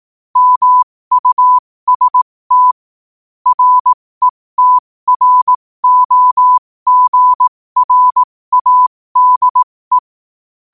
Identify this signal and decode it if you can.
Morse code, heard as a keyed tone, about 18 wpm, spaced as for 15 wpm, 1000 Hz. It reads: MUST RETROGRADE
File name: qotdfiles / 15wpm